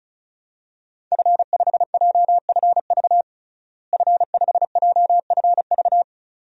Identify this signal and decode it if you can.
Morse code, heard as a keyed tone, about 36 wpm, 700 Hz. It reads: F5JFV F5JFV
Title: DXCC-100 Call Signs - 5 Characters - Repeated 2x - 35wpm